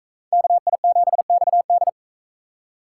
Morse code, heard as a keyed tone, about 42 wpm, 700 Hz. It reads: KI7XD